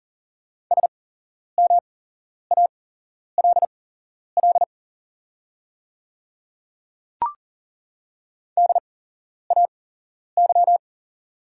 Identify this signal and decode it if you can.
Morse code, heard as a keyed tone, about 40 wpm, spaced as for 11 wpm, 690 Hz. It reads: SMALL DAY
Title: 8x Character Spacing - Top 200 Words - 40wpm